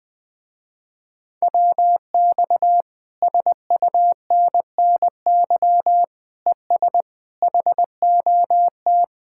Morse code, heard as a keyed tone, about 20 wpm, 700 Hz. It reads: WX SUNNY ES HOT